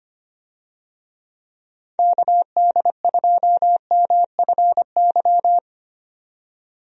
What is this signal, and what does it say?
Morse code, heard as a keyed tone, about 25 wpm, 700 Hz. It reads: KD2MFY
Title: USA Call Signs - 6 Characters - 25wpm